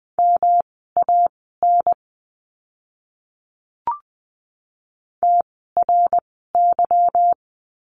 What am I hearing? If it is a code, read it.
Morse code, heard as a keyed tone, about 20 wpm, spaced as for 14 wpm, 700 Hz. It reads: MAN TRY
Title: 160 Lesson 26f - Letter g - Words 20wpm